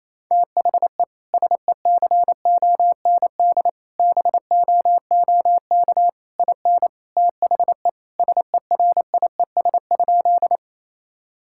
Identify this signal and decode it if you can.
Morse code, heard as a keyed tone, about 27 wpm, 700 Hz. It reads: THE SECOND BOOK IN THE SERIES?
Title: Sentences from Top 1000 Words - Review Part 5 - 28wpm